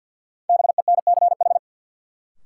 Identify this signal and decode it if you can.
Morse code, heard as a keyed tone, about 53 wpm, 700 Hz. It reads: BENCH
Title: Sentences from Top 1400 Words - New Word Drill Alt - 50wpm